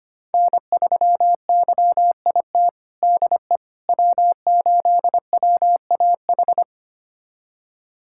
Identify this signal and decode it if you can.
Morse code, heard as a keyed tone, about 25 wpm, 700 Hz. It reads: N3YIT DE W8WAH
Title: USA Callsign DE Callsign - 5 Characters - 25wpm